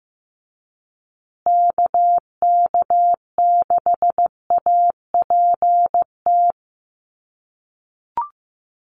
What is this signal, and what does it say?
Morse code, heard as a keyed tone, about 15 wpm, 700 Hz. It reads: KK6APT
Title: USA Call Signs Repeated 2x 15wpm